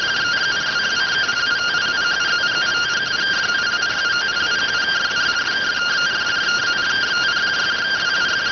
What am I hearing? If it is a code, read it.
Two-tone teleprinter signal, about 170 Hz shift.